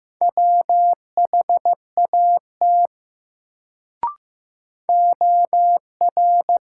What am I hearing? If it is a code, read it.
Morse code, heard as a keyed tone, about 16 wpm, 700 Hz. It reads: WHAT OR
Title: Sentences from Top 100 Words - New Word Drill - 15wpm